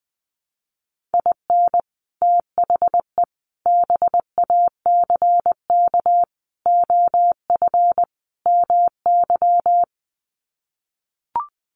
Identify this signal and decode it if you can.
Morse code, heard as a keyed tone, about 20 wpm, 700 Hz. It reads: IN THE BACK OF MY
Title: Sets of 5 Words Encore 20wpm